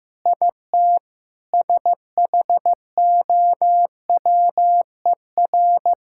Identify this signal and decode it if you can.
Morse code, heard as a keyed tone, about 15 wpm, 700 Hz. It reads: IT SHOWER
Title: Sentences from Top 1600 Words 15wpm